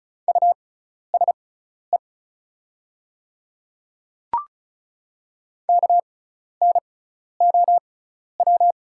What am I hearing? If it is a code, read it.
Morse code, heard as a keyed tone, about 34 wpm, spaced as for 12 wpm, 700 Hz. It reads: USE KNOW
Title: 6x Character Spacing - Top 100 Words - 35wpm